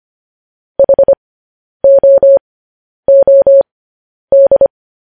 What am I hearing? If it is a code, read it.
Morse code, heard as a keyed tone, about 25 wpm, spaced as for 10 wpm, 550 Hz. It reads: HOOD